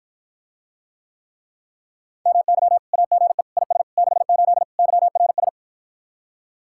Morse code, heard as a keyed tone, about 54 wpm, 700 Hz. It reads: MY AGE IS 68 YRS